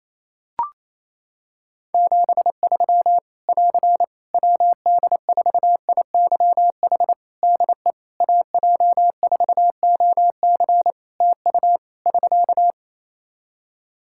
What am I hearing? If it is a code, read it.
Morse code, heard as a keyed tone, about 28 wpm, 700 Hz. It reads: 73 <AR> WD4IYH DE AJ4OC TU <SK>
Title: Practice QSO - Exchange 4 - Ending - 28wpm